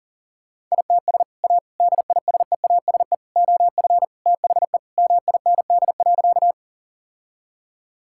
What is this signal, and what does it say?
Morse code, heard as a keyed tone, about 40 wpm, 700 Hz. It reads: ITS A DISEASE OF THE MIND.
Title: Sentences from Top 1000 Words - Review Part 6 - 40wpm